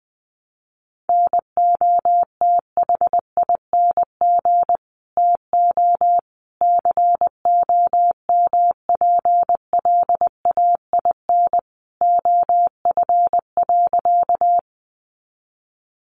Morse code, heard as a keyed tone, about 19 wpm, 700 Hz. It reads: NOTHING TO COMPLAIN OF.
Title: Sentences from Top 1300 Words 20wpm